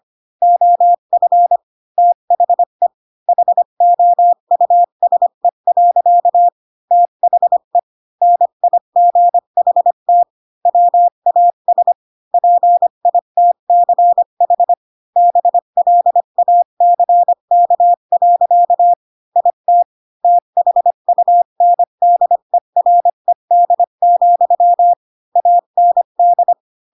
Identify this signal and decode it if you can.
Morse code, heard as a keyed tone, about 25 wpm, 700 Hz. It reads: OF THE HOUSE. THE NIGHT WAS PITCH BLACK. IT THUNDERED, AND